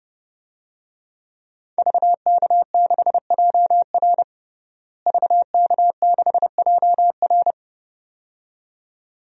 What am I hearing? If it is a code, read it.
Morse code, heard as a keyed tone, about 31 wpm, 700 Hz. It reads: VK6JR VK6JR
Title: DXCC-100 Call Signs - 5 Characters - Repeated 2x - 30wpm